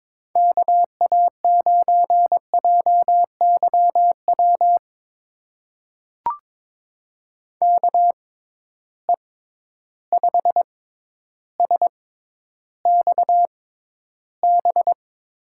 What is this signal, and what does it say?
Morse code, heard as a keyed tone, about 21 wpm, spaced as for 11 wpm, 700 Hz. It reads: KA9JYW KE5SXB
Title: USA Call Signs - 6 Characters - 6x Character Spacing - 22wpm